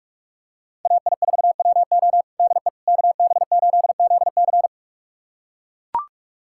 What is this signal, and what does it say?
Morse code, heard as a keyed tone, about 41 wpm, 700 Hz. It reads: AI4WO DE KB8ZC
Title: USA Callsign DE Callsign - 5 Characters - 45wpm